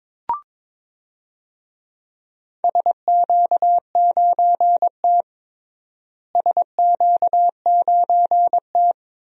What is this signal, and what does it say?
Morse code, heard as a keyed tone, about 22 wpm, 700 Hz. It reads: SQ9T SQ9T
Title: DXCC-100 Call Signs - 4 Characters - Repeated 2x - 22wpm